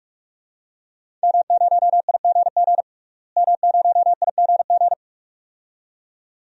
Morse code, heard as a keyed tone, about 43 wpm, 700 Hz. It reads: M0IGG M0IGG
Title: DXCC-100 Call Signs - 5 Characters - Repeated 2x - 45wpm